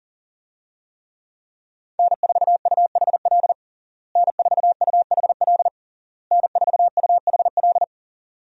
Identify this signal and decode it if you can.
Morse code, heard as a keyed tone, about 41 wpm, 700 Hz. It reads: N4UHL N4UHL N4UHL
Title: USA Call Signs - 5 Characters - Repeated 3x - 40wpm